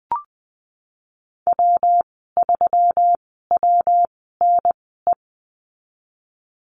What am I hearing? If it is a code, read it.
Morse code, heard as a keyed tone, about 20 wpm, spaced as for 15 wpm, 700 Hz. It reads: W3WNE
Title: 128 Lesson 21g - Number 6 - Call Signs 20wpm